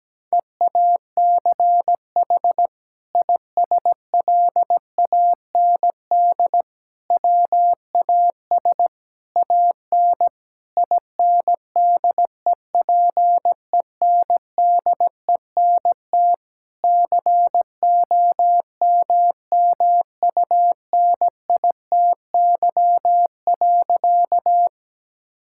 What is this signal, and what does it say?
Morse code, heard as a keyed tone, about 17 wpm, 700 Hz. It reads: EACH ISLAND WAS AN INDEPENDENT COMMUNITY.
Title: Sentences from Top 1000 Words - Review Part 6 - 17wpm